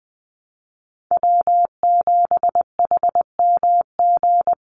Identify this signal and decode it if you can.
Morse code, heard as a keyed tone, about 19 wpm, 700 Hz. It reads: W7HMG
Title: USA Call Signs 20wpm